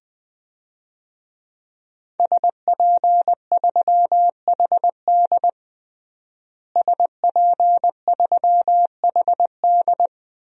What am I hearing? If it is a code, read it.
Morse code, heard as a keyed tone, about 20 wpm, 700 Hz. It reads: SP3HD SP3HD
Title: DXCC-100 Call Signs - 5 Characters - Repeated 3x - 20wpm